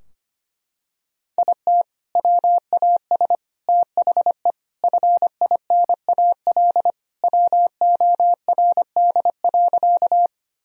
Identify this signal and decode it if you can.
Morse code, heard as a keyed tone, about 25 wpm, 700 Hz. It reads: IT WAS THE FINAL WORD.
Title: Sentences from Top 1000 Words - Review Part 2 - 25wpm